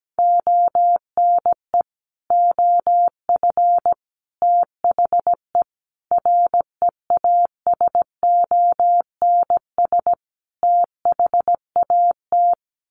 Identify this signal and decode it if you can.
Morse code, heard as a keyed tone, about 17 wpm, 700 Hz. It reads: ONE OF THE REASONS THAT
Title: Sets of 5 Words Encore 17wpm